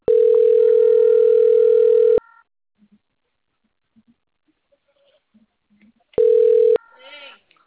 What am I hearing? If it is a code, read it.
Ringback tone (North America), 440 + 480 Hz, with speech.
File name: out- - - - - .87041